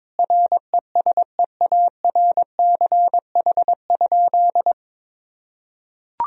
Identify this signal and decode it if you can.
Morse code, heard as a keyed tone, about 22 wpm, 700 Hz. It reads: RESEARCH?
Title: Sentences from Top 1000 Words - Review Part 4 - 22wpm